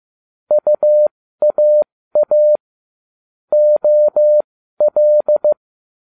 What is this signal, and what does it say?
Morse code, heard as a keyed tone, about 15 wpm, spaced as for 13 wpm, 600 Hz. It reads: UAA OL